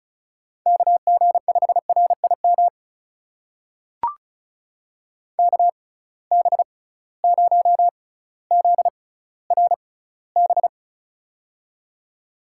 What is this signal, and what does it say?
Morse code, heard as a keyed tone, about 36 wpm, spaced as for 18 wpm, 690 Hz. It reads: KG5RIM KB0ZRB
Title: USA Call Signs - 6 Characters - 6x Character Spacing - 35wpm